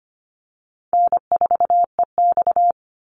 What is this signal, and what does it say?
Morse code, heard as a keyed tone, about 25 wpm, 700 Hz. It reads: N4EX
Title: NA SOTA Activator Callsigns 25wpm